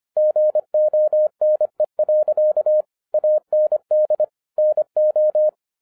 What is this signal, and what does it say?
Morse code, heard as a keyed tone, about 25 wpm, 610 Hz. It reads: GONE. AND NO